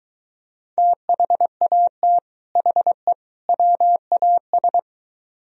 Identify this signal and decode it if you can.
Morse code, heard as a keyed tone, about 23 wpm, 700 Hz. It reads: THAT HE WAS